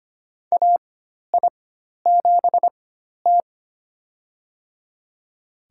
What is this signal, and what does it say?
Morse code, heard as a keyed tone, about 25 wpm, spaced as for 12 wpm, 690 Hz. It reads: AI7T
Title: USA Call Signs - 4 Characters - 4x Character Spacing - 25wpm